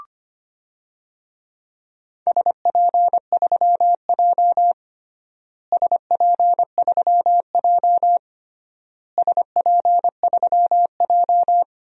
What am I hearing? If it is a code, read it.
Morse code, heard as a keyed tone, about 25 wpm, 700 Hz. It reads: SP3J SP3J SP3J
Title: DXCC-100 Call Signs - 4 Characters - Repeated 3x - 25wpm